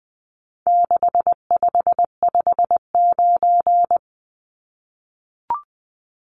Morse code, heard as a keyed tone, about 20 wpm, 700 Hz. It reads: 6559
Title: Numbers - 4-Digit 20wpm